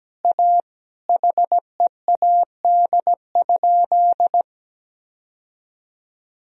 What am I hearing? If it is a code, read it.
Morse code, heard as a keyed tone, about 17 wpm, 700 Hz. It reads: A HEAD?
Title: Sentences from Top 1000 Words - Review Part 3 - 17wpm